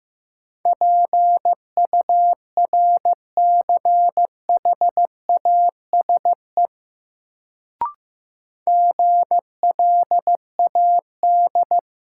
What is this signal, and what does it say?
Morse code, heard as a keyed tone, about 15 wpm, 700 Hz. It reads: PURCHASE GLAD
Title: Sentences from Top 800 Words - New Word Drill - 15wpm